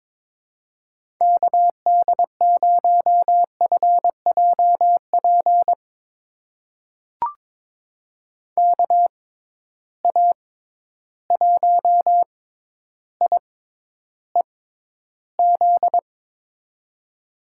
Morse code, heard as a keyed tone, about 21 wpm, spaced as for 11 wpm, 700 Hz. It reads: KD0FJP KA1IEZ
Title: USA Call Signs - 6 Characters - 6x Character Spacing - 22wpm